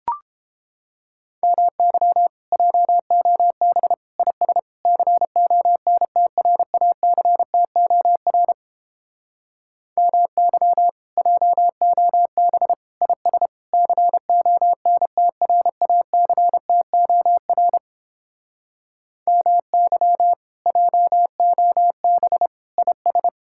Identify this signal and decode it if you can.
Morse code, heard as a keyed tone, about 31 wpm, 700 Hz. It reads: MY JOB IS CONTRACTOR MY JOB IS CONTRACTOR MY JOB IS